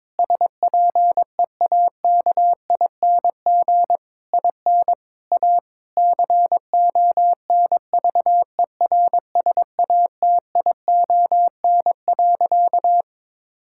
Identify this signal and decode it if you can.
Morse code, heard as a keyed tone, about 22 wpm, 700 Hz. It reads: SPEAKING IN A CONVERSATION.
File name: Difficult Sentences with Hints - 22wpm